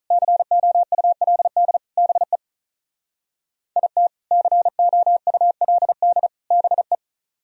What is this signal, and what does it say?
Morse code, heard as a keyed tone, about 44 wpm, 700 Hz. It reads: COULD BE IT COULD BE
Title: Speed-Racing - Sets of 3 Words 35wpm